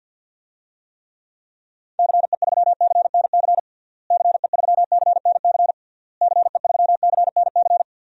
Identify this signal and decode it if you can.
Morse code, heard as a keyed tone, about 52 wpm, 700 Hz. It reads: CE3CNC CE3CNC CE3CNC
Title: DXCC-100 Call Signs - 6 Characters - Repeated 3x - 50wpm